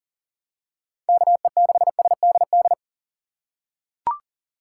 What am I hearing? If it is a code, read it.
Morse code, heard as a keyed tone, about 40 wpm, 700 Hz. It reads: KE6SDD
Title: USA Call Signs - 6 Characters - 3x Character Spacing - 40wpm